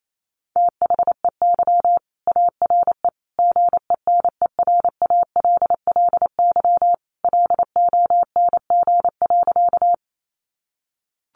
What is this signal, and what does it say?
Morse code, heard as a keyed tone, about 28 wpm, 700 Hz. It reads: THEY ARE GENERALLY LONG.
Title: Sentences from Top 600 Words 28wpm